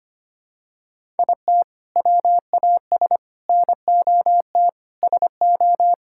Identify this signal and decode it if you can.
Morse code, heard as a keyed tone, about 25 wpm, 700 Hz. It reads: IT WAS NOT SO
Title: Sentences from Top 1600 Words 25wpm